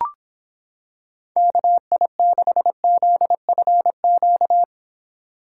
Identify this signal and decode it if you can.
Morse code, heard as a keyed tone, about 26 wpm, 700 Hz. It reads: KI6ZFQ